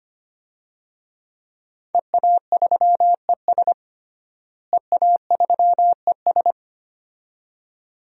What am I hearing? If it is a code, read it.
Morse code, heard as a keyed tone, about 25 wpm, 700 Hz. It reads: EA3ES EA3ES
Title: DXCC-100 Call Signs - 5 Characters - Repeated 2x - 25wpm